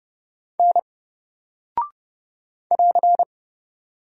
Morse code, heard as a keyed tone, about 29 wpm, 700 Hz. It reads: N <AR>
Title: Single Character (Letter:Number:Punctuation:Prosign) 30wpm